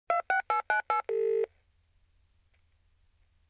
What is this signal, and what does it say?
DTMF, heard as telephone touch tones, tones of about 120 ms. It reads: 26898
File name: prev